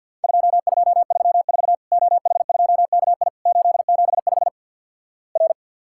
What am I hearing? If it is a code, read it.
Morse code, heard as a keyed tone, about 48 wpm, 700 Hz. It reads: 2234 OH2KI 875 R
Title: Worked All Europe DX Contest - QTC Exchange - 50wpm